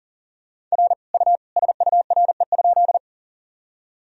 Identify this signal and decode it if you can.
Morse code, heard as a keyed tone, about 40 wpm, 700 Hz. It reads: R U SURE?
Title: Sentences from Top 300 Words and Abbreviations 40wpm